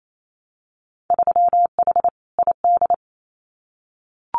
Morse code, heard as a keyed tone, about 28 wpm, 700 Hz. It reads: 3H ID
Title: Winter Field Day Exchanges Encore 28wpm